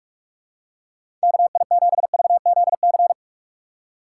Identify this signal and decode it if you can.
Morse code, heard as a keyed tone, about 43 wpm, 700 Hz. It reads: KI7VZC
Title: USA Call Signs - 6 Characters - 6x Character Spacing - 45wpm